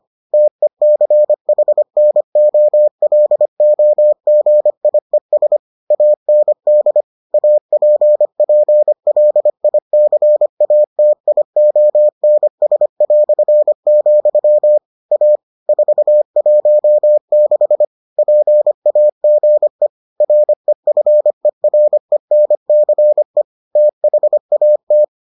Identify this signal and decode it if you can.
Morse code, heard as a keyed tone, about 25 wpm, 600 Hz. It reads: TECHNOLOGIES AND APPLICATIONS", A 416 PAGE REFERENCE THAT